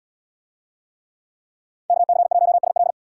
Morse code, heard as a keyed tone, about 77 wpm, 700 Hz. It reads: KC1SL